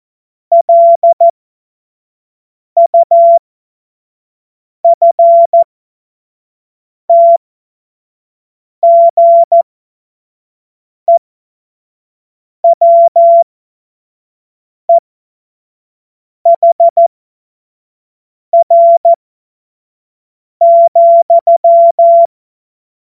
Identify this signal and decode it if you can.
Morse code, heard as a keyed tone, about 14 wpm, spaced as for 5 wpm, 680 Hz. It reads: LUFTGEWEHR,
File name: Morsekurs / Witze / Witz 15. August 2023 25 F70